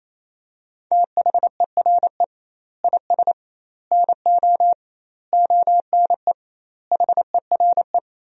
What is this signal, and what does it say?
Morse code, heard as a keyed tone, about 28 wpm, 700 Hz. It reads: THERE IS NO ONE HERE
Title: 2x Word-Spacing - Sentences from Top 200 Words - 28wpm